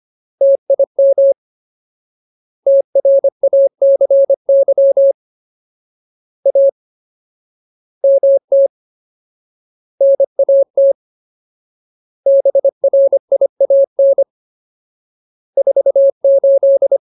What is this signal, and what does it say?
Morse code, heard as a keyed tone, about 25 wpm, 550 Hz. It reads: TIM TRACY A MT NAT BRIAN 48
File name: Ultimate Word List / 25WPM Ultimate